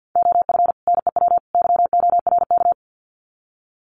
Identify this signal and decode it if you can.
Morse code, heard as a keyed tone, about 50 wpm, 700 Hz. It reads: OF NEW YORK
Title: Sets of 3 Words 50wpm